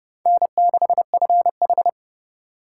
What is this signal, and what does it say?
Morse code, heard as a keyed tone, about 30 wpm, 700 Hz. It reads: N6FH
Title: USA Call Signs - 2x Character Spacing - 30wpm